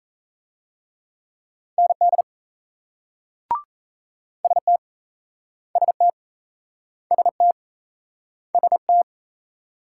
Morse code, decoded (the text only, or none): ND ST ST ST ST